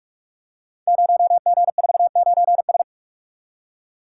Morse code, heard as a keyed tone, about 44 wpm, 690 Hz. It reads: 0G49S
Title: Code Groups - 5 Letter:Numbers - 45wpm